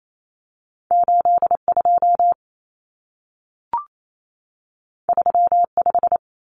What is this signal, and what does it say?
Morse code, heard as a keyed tone, about 28 wpm, 700 Hz. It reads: 82 35
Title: Common Cut Numbers - 2-Digit 28wpm